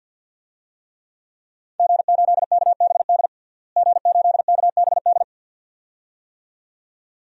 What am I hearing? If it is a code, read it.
Morse code, heard as a keyed tone, about 51 wpm, 700 Hz. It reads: G8KBD G8KBD
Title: DXCC-100 Call Signs - 5 Characters - Repeated 2x - 50wpm